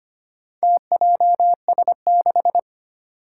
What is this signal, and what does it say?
Morse code, heard as a keyed tone, about 25 wpm, 700 Hz. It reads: TJS6
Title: Four Challenging Letter-Number Combinations 25wpm